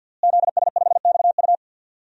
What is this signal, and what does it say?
Morse code, heard as a keyed tone, about 52 wpm, 700 Hz. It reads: ZS5XU
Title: DXCC-100 Call Signs - 5 Characters - Repeated 3x - 50wpm